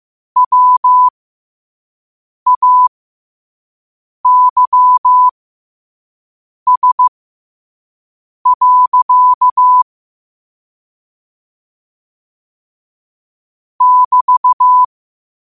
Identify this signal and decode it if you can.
Morse code, heard as a keyed tone, about 15 wpm, spaced as for 5 wpm, 1000 Hz. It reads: WAYS. <BT>